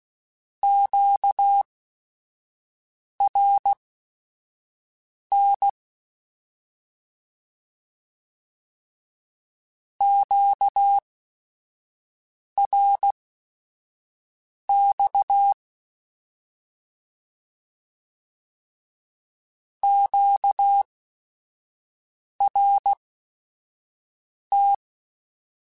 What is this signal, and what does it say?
Morse code, heard as a keyed tone, about 16 wpm, spaced as for 5 wpm, 800 Hz. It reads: QRN QRX QRT
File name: Short Runs / Short Runs-mp3s / Words @5